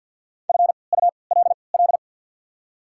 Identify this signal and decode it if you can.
Morse code, heard as a keyed tone, about 53 wpm, spaced as for 29 wpm, 700 Hz. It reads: FULL
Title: 3x Character Spacing - Top 500 Words - 50wpm